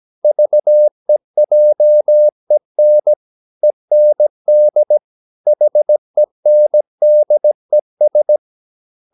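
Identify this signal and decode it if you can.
Morse code, heard as a keyed tone, about 17 wpm, 600 Hz. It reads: VEJEN END HENDES
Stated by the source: Slow (QRS) Morse practice at 17 wpm, from CW med Gnister.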